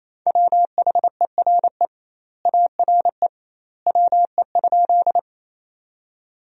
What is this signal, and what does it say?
Morse code, heard as a keyed tone, about 28 wpm, 700 Hz. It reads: WHERE ARE WE?
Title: 2x Word-Spacing - Sentences from Top 200 Words - 28wpm